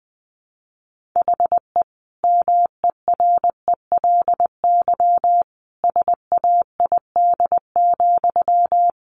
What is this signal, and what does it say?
Morse code, heard as a keyed tone, about 20 wpm, 700 Hz. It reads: HE MERELY SAID,
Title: Sentences from Top 1400 Words 20wpm